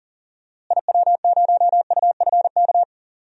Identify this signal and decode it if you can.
Morse code, heard as a keyed tone, about 38 wpm, 700 Hz. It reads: IW0UFK
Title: DXCC-100 Call Signs - 6 Characters - Repeated 2x - 40wpm